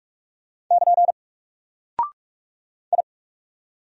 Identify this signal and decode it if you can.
Morse code, heard as a keyed tone, about 44 wpm, 700 Hz. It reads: <KN> I
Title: Single Character (Letter:Number:Punctuation:Prosign) Alt 45wpm